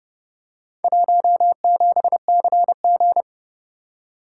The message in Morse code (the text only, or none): 17CG